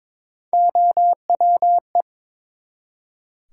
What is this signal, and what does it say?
Morse code, heard as a keyed tone, about 23 wpm, 700 Hz. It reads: OWE
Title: Three Letter Words - 22wpm